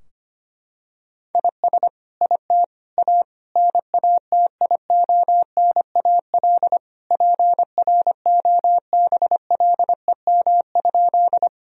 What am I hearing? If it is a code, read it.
Morse code, heard as a keyed tone, about 25 wpm, 700 Hz. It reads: IS IT A NATIONAL PROBLEM?
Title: Sentences from Top 1000 Words - Review Part 4 - 25wpm